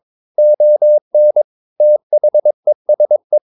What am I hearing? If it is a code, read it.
Morse code, heard as a keyed tone, about 22 wpm, 600 Hz. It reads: ON THESE
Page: CW med Gnister